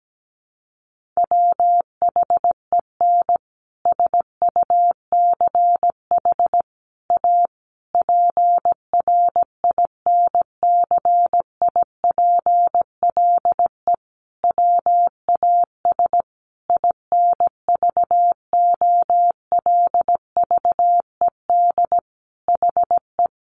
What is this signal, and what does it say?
Morse code, heard as a keyed tone, about 17 wpm, 700 Hz. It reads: WHEN SUCH A PRINCIPLE WAS INVOLVED HE
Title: Sentences from Top 1400 Words 17wpm